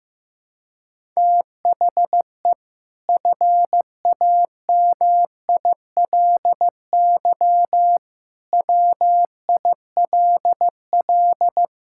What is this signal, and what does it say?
Morse code, heard as a keyed tone, about 15 wpm, 700 Hz. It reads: THE FAMILY WILL
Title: Sentences from Top 1000 Words - Review Part 6 - 15wpm